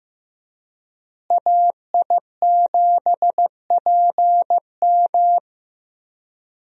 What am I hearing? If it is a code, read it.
Morse code, heard as a keyed tone, about 16 wpm, 700 Hz. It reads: AI7PM
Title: USA Call Signs - 5 Characters - Repeated 3x - 15wpm